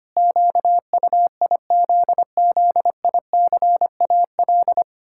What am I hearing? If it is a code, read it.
Morse code, heard as a keyed tone, about 25 wpm, 700 Hz. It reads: QUIZZICAL
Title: Pangram Challenge v2 - 25wpm